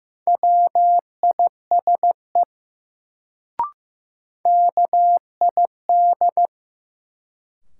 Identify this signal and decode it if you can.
Morse code, heard as a keyed tone, about 15 wpm, 700 Hz. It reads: WISE KID
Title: Sentences from Top 1600 Words - New Word Drill - 15wpm